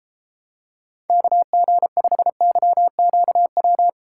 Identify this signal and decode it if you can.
Morse code, heard as a keyed tone, about 29 wpm, 700 Hz. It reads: KG5YQW